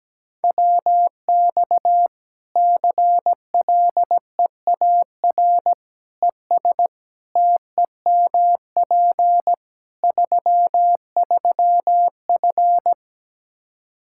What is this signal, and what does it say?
Morse code, heard as a keyed tone, about 17 wpm, 700 Hz. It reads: WX CLEAR ES TEMP 33F
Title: Practice QSO - Exchange 2 - Info - 17wpm